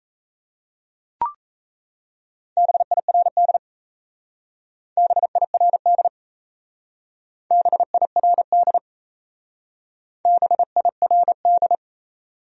Morse code, heard as a keyed tone, about 35 wpm, 700 Hz. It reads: BIRD BIRD BIRD BIRD
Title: Speed-Racing Top 500 Words 28wpm